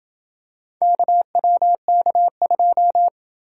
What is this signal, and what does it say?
Morse code, heard as a keyed tone, about 27 wpm, 700 Hz. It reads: KWK2